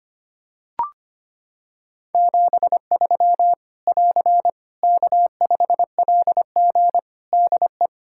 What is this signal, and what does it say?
Morse code, heard as a keyed tone, about 25 wpm, 700 Hz. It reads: 73 <AR> K5LG DE
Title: K1USN SST QSO Practice 25wpm